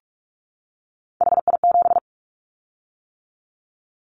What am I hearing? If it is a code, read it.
Morse code, heard as a keyed tone, about 46 wpm, 700 Hz. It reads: HI7
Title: Callsign Prefixes for Top 100 DXCC - 45wpm